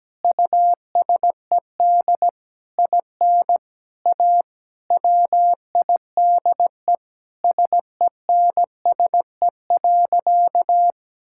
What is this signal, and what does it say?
Morse code, heard as a keyed tone, about 17 wpm, 700 Hz. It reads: USED IN A WIDE SENSE.
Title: Sentences from Top 1000 Words - Review Part 6 - 17wpm